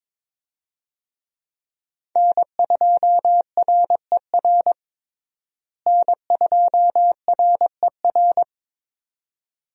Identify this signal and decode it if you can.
Morse code, heard as a keyed tone, about 22 wpm, 700 Hz. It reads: N2RER N2RER
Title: USA Call Signs - 5 Characters - Repeated 2x - 22wpm